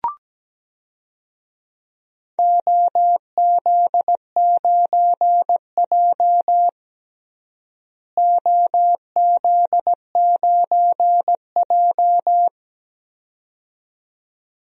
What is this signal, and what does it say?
Morse code, heard as a keyed tone, about 17 wpm, 700 Hz. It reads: OZ9J OZ9J
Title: DXCC-100 Call Signs - 4 Characters - Repeated 2x - 17wpm